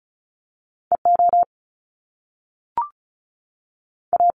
Morse code, heard as a keyed tone, about 36 wpm, 700 Hz. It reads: EO A
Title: Top Two Letter Combinations in Words 35wpm